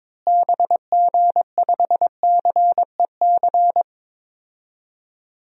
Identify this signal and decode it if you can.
Morse code, heard as a keyed tone, about 22 wpm, 700 Hz. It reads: BG5CEC
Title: DXCC-100 Call Signs - 6 Characters - Repeated 3x - 22wpm